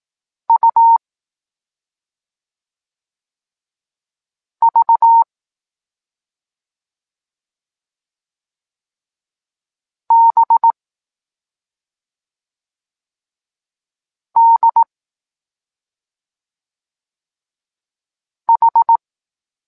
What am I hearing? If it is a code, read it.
Morse code, heard as a keyed tone, about 18 wpm, 920 Hz. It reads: U V B D H
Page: Tekst 14 – doskonalenie odbioru poznanych liter